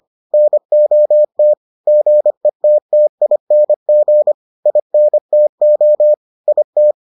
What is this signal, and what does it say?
Morse code, heard as a keyed tone, about 25 wpm, 600 Hz. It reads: NOT GETTING INTO IT